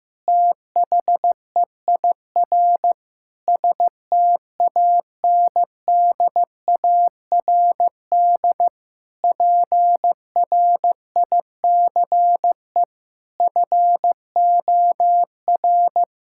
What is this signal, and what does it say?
Morse code, heard as a keyed tone, about 15 wpm, 700 Hz. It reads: THEIR STANDARD PRICE FOR
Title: Sentences from Top 1000 Words - Review Part 6 - 15wpm